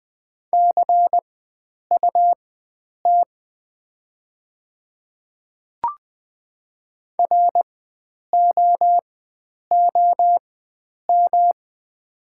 Morse code, heard as a keyed tone, about 20 wpm, spaced as for 9 wpm, 690 Hz. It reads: CUT ROOM
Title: 4x Character Spacing - Top 300 Words - 20wpm